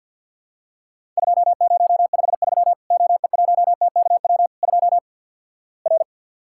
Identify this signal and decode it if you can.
Morse code, heard as a keyed tone, about 51 wpm, 700 Hz. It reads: 2053 OE1TKW 2 R
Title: Worked All Europe DX Contest - QTC Exchange - 50wpm